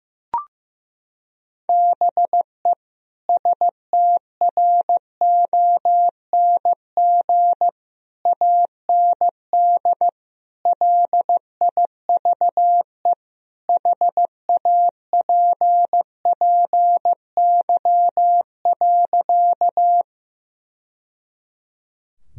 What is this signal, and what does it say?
Morse code, heard as a keyed tone, about 14 wpm, 700 Hz. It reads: BE STRONG AND LIVE HAPPY.
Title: Sentences from Top 1000 Words - Review Part 3 - 15wpm